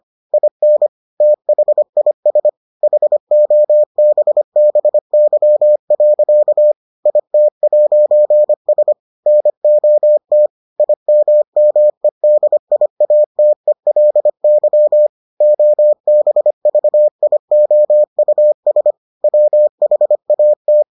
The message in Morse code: IN THIS HOBBY. IT'S NOT IMMEDIATELY OBVIOUS WHAT

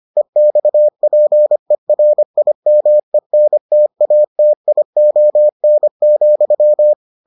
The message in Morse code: EXPERIMENTATION,

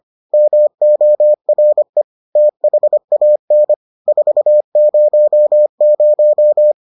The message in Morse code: MORE THAN 400

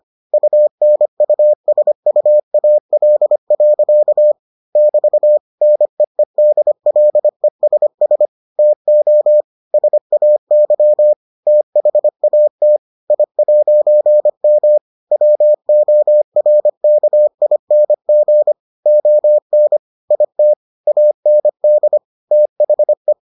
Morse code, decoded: UNUSUAL. <BT> NEEDLESS TO SAY THAT I'M WORKING ON IT AND THE

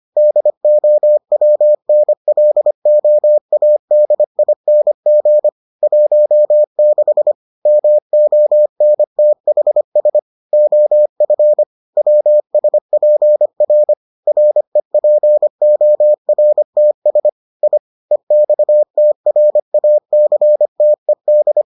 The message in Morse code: DOWNLOADING 16 MONTHS OF WSPR REPORTS I EXTRACTED

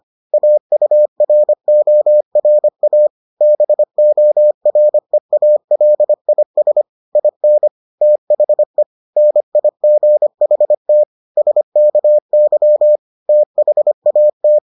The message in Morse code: AURORA BOREALIS IN THE NIGHT SKY THAT

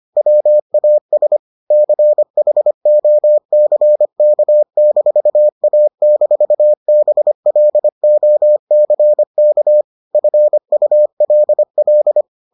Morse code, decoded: WAS CHOCK-A-BLOCK FULL

